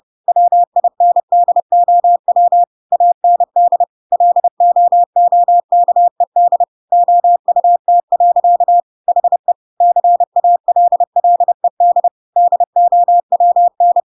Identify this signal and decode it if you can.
Morse code, heard as a keyed tone, about 30 wpm, 700 Hz. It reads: WINDOW AND LOOKED OUT. HE CALLED DOWN
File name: Pinocchio / Pinocchio ch6 30 30 ik1hge